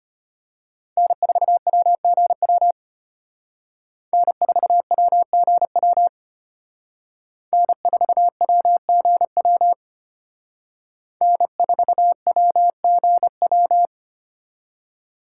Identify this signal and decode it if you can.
Morse code, heard as a keyed tone, about 32 wpm, 700 Hz. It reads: N4WGW N4WGW N4WGW N4WGW